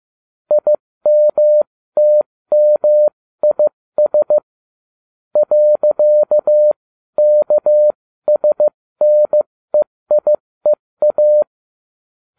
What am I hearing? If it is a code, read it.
Morse code, heard as a keyed tone, about 15 wpm, spaced as for 13 wpm, 600 Hz. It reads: IMTMIS .KSNEIEA